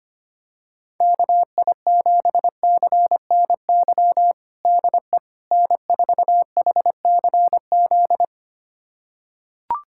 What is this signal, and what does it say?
Morse code, heard as a keyed tone, about 25 wpm, 700 Hz. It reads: KI7CNY DE N4HCZ
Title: USA Callsign DE Callsign 25wpm